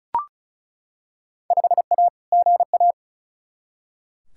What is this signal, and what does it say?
Morse code, heard as a keyed tone, about 36 wpm, 700 Hz. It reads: 5A GA
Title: Field Day Exchanges Encore - 35wpm